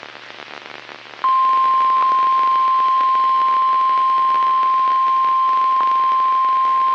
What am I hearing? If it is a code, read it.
Emergency Alert System (SAME) tones (the 1050 Hz attention tone). It is heard from 1.2 to 5.8 s.
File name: eas messages / EAS WXL38 Binghamton